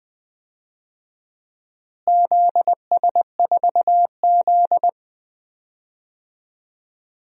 Morse code, heard as a keyed tone, about 20 wpm, 700 Hz. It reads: ZS4Z
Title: DXCC-100 Call Signs - 4 Characters - Repeated 3x - 20wpm